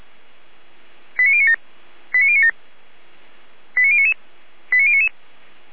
Five-tone selective call (ZVEI), heard as quick runs of five tones, tones of about 70 ms. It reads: 89097 89097 89099 89099
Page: SDR/ZVEI